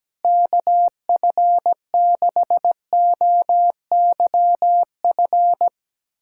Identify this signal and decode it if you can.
Morse code, heard as a keyed tone, about 17 wpm, 700 Hz. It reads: KF6OYF
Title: USA Callsign DE Callsign - 6 Characters - 17wpm